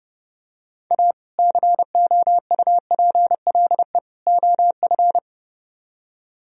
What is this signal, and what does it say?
Morse code, heard as a keyed tone, about 30 wpm, 700 Hz. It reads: A COUPLE OF